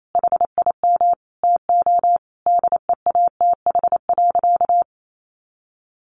Morse code, heard as a keyed tone, about 28 wpm, 700 Hz. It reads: HIM TO DEATH.
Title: Sentences from Top 1100 Words 28wpm